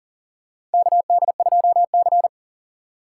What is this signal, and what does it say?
Morse code, heard as a keyed tone, about 37 wpm, 700 Hz. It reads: KD2C
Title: USA Call Signs - 4 Characters - 3x Character Spacing - 40wpm